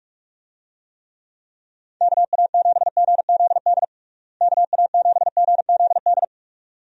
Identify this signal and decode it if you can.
Morse code, heard as a keyed tone, about 46 wpm, 700 Hz. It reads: KA7GZD KA7GZD
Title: USA Call Signs - 6 Characters - Repeated 2x - 45wpm